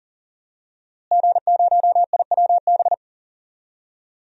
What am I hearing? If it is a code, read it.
Morse code, heard as a keyed tone, about 40 wpm, 700 Hz. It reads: G0IWB
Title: DXCC-100 Call Signs - 5 Characters - Repeated 2x - 40wpm